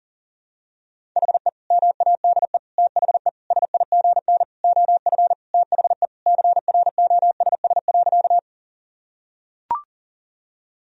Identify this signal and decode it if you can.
Morse code, heard as a keyed tone, about 40 wpm, 700 Hz. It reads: HE MADE THE SIGN OF THE CROSS.
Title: Sentences from Top 1000 Words - Review Part 1 - 40wpm